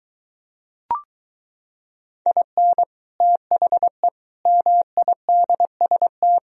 Morse code, heard as a keyed tone, about 23 wpm, 700 Hz. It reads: IN THE MIDST